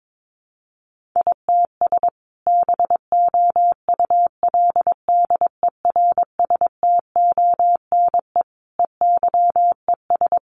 Morse code, heard as a keyed tone, about 22 wpm, 700 Hz. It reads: ITS BOULDERSTONE EYES
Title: A Sound Of Thunder 22wpm